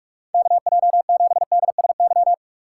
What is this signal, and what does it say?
Morse code, heard as a keyed tone, about 44 wpm, 700 Hz. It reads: KJ7DSY
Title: USA Call Signs - 6x Character Spacing - 45wpm